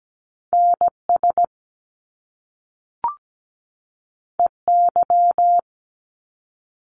Morse code, heard as a keyed tone, about 17 wpm, 700 Hz. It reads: NS EY
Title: Top Two Letter Combinations in Words 17wpm